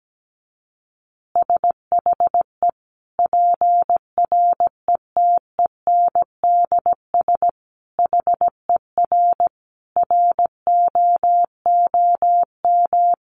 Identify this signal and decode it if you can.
Morse code, heard as a keyed tone, about 17 wpm, 700 Hz. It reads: SHE PRETENDS HER ROOM
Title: Additional Easy Phrases 17wpm